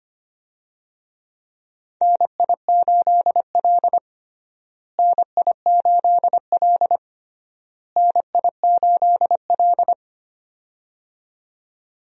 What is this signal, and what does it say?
Morse code, heard as a keyed tone, about 25 wpm, 700 Hz. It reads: NI8L NI8L NI8L
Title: USA Call Signs - 4 Characters - Repeated 3x - 25wpm